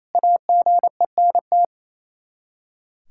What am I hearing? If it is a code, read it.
Morse code, heard as a keyed tone, about 28 wpm, 700 Hz. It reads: AGENT
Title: K1USN SST QSO Practice 28wpm